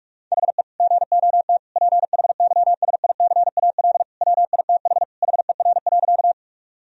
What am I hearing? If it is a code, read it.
Morse code, heard as a keyed tone, about 46 wpm, 700 Hz. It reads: HE GOT PHYSICAL WITH HER.
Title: Sentences from Top 1000 Words - Review Part 4 - 45wpm